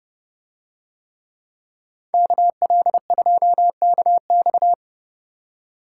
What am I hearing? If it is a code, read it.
Morse code, heard as a keyed tone, about 28 wpm, 700 Hz. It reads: KL2KX